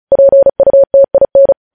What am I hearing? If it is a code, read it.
Morse code, heard as a keyed tone, about 36 wpm, 550 Hz. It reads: PUTIN
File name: podcast / News headlines-1x-35x35